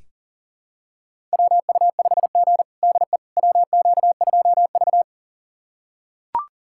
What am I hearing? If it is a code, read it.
Morse code, heard as a keyed tone, about 37 wpm, 700 Hz. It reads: WU5G DE WQ2V
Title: USA Callsign DE Callsign - 4 Characters - 40wpm